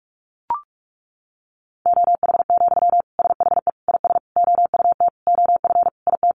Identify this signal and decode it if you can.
Morse code, heard as a keyed tone, about 44 wpm, 700 Hz. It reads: OH, SHE IS OUT OF IT
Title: Sentences from Top 1100 Words 45wpm